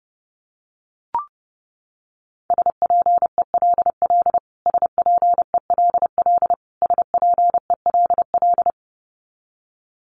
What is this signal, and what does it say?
Morse code, heard as a keyed tone, about 30 wpm, 700 Hz. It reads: SPELL SPELL SPELL
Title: Top 200 Words Repeated 30wpm